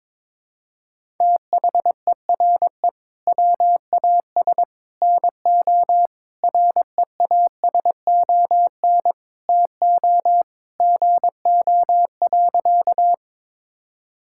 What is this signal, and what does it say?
Morse code, heard as a keyed tone, about 21 wpm, 700 Hz. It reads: THERE WAS NO REASON TO GO.
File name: practice / 22wpm / Sentences with 3-Word Phrases - 22wpm